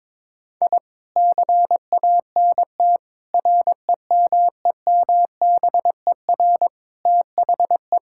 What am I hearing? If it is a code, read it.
Morse code, heard as a keyed tone, about 22 wpm, 700 Hz. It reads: I CANT REMEMBER THE
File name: practice / 22wpm / Sets of 7 Words - 22wpm